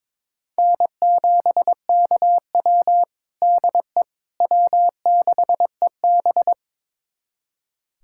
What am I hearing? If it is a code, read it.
Morse code, heard as a keyed tone, about 21 wpm, 700 Hz. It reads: N7KW DE W6EB
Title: USA Callsign DE Callsign - 4 Characters - 22wpm